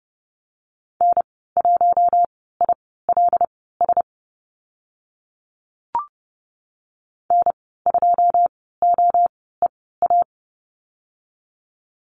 Morse code, heard as a keyed tone, about 30 wpm, spaced as for 17 wpm, 700 Hz. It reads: N1ILS N2OEA
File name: Lesson 14g - Number 2 - Call Signs 30wpm